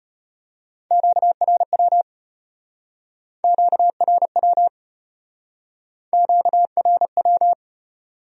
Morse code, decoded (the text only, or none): QRW QRW QRW